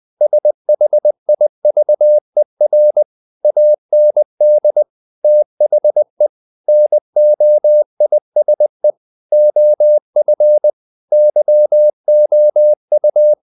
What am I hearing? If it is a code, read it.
Morse code, heard as a keyed tone, about 20 wpm, 600 Hz. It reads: SHIVER AND THE NOISE OF YOU
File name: the old man and the sea / the old man and the sea part 1 20wpm